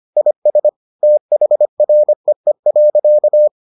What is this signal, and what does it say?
Morse code, heard as a keyed tone, about 25 wpm, 600 Hz. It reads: IS THREE.